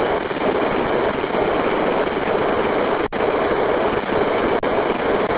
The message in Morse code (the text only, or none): none